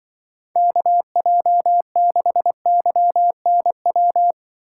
Morse code, heard as a keyed tone, about 25 wpm, 700 Hz. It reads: KJ6YNW
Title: Speed Racing - USA Call Signs 20wpm